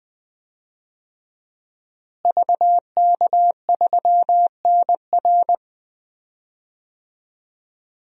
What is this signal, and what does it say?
Morse code, heard as a keyed tone, about 20 wpm, 700 Hz. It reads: VK3NR
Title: DXCC-100 Call Signs - 5 Characters - Repeated 3x - 20wpm